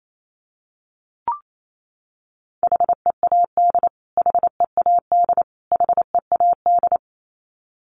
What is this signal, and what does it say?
Morse code, heard as a keyed tone, about 28 wpm, 700 Hz. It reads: HEAD HEAD HEAD
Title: Top 500 Words Repeated 28wpm